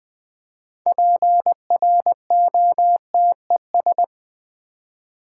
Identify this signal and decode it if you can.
Morse code, heard as a keyed tone, about 20 wpm, 700 Hz. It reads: PROTES
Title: Build A Word - 3 Letters At A Time - 9-Letter Words - 20wpm